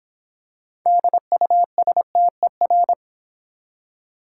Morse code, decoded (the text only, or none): DUSTER